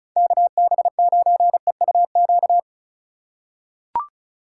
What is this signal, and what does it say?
Morse code, heard as a keyed tone, about 36 wpm, 700 Hz. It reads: KB9EUQ
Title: USA Call Signs - 6 Characters - 4x Character Spacing - 35wpm